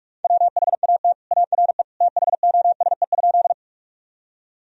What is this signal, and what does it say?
Morse code, heard as a keyed tone, about 46 wpm, 700 Hz. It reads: WHAT ARE THOSE?
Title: Sentences from Top 1000 Words - Review Part 1 - 45wpm